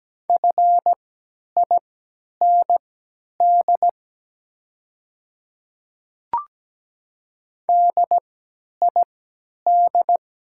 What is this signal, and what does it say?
Morse code, heard as a keyed tone, about 17 wpm, spaced as for 10 wpm, 700 Hz. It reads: FIND DID
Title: 3x Character Spacing - Top 100 Words - 17wpm